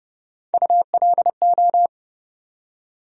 Morse code, heard as a keyed tone, about 30 wpm, 700 Hz. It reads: ULO
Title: USA Call Sign Postfixes - 3 Letters 30wpm